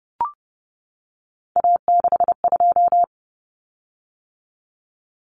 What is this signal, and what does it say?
Morse code, heard as a keyed tone, about 30 wpm, 700 Hz. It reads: A62
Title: Common Cut Numbers - 3-Digit 30wpm